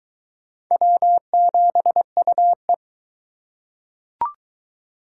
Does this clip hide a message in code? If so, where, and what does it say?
Yes: Morse code from 0.7 to 2.7 s, heard as a keyed tone, about 22 wpm, 700 Hz. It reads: W7UE